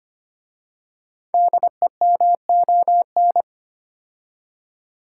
Morse code, heard as a keyed tone, about 25 wpm, 700 Hz. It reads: DEMON
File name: 5 Letter Words - First 3 Letters Given - 25wpm